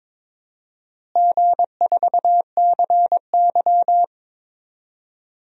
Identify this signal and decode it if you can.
Morse code, heard as a keyed tone, about 22 wpm, 700 Hz. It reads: G4CY
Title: DXCC-100 Call Signs - 4 Characters - Repeated 2x - 22wpm